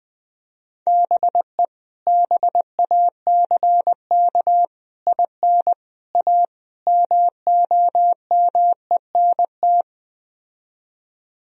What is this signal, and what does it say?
Morse code, heard as a keyed tone, about 20 wpm, 700 Hz. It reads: BE BACK IN A MOMENT
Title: Speed-Racing - Sets of 5 Words 17wpm